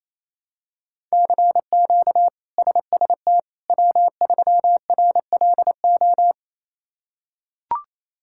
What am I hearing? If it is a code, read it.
Morse code, heard as a keyed tone, about 28 wpm, 700 Hz. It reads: CQ SST W3RLO
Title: K1USN SST QSO Practice - 28wpm